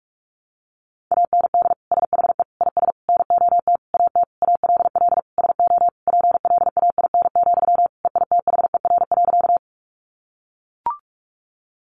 Morse code, heard as a keyed tone, about 44 wpm, 700 Hz. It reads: AND SHE IS NOT AT ALL SO PLAIN, EITHER.
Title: Sentences from Top 600 Words 45wpm